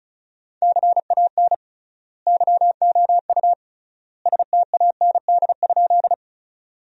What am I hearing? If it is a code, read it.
Morse code, heard as a keyed tone, about 36 wpm, 700 Hz. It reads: CAN YOU STAND?
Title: 3x Word-Spacing - Sentences from Top 300 Words - 35wpm